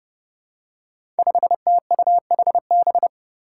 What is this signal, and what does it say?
Morse code, heard as a keyed tone, about 29 wpm, 700 Hz. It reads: 5TUHB
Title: Five Challenging Letter-Number Combinations 30wpm